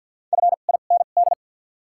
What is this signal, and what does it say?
Morse code, heard as a keyed tone, about 54 wpm, 700 Hz. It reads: F I N D
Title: 2x Character Spacing - Top 100 Words - 50wpm